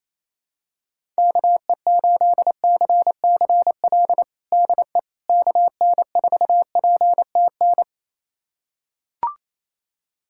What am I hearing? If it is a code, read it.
Morse code, heard as a keyed tone, about 27 wpm, 700 Hz. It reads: KE8CCL DE KN4PTN
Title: USA Callsign DE Callsign - 6 Characters - 28wpm